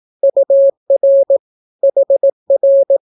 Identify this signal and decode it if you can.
Morse code, heard as a keyed tone, about 18 wpm, 550 Hz. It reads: UR HR
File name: naukapodstaw / skroty / skroty